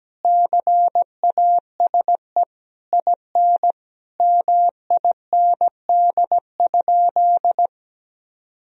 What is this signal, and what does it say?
Morse code, heard as a keyed tone, about 17 wpm, 700 Hz. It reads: CASE IN MIND?
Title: Sentences from Top 1000 Words - Review Part 4 - 17wpm